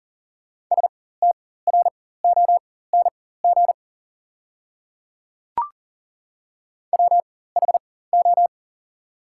Morse code, heard as a keyed tone, about 37 wpm, spaced as for 18 wpm, 700 Hz. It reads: STRONG WHO
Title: 4x Character Spacing - Top 500 Words - 40wpm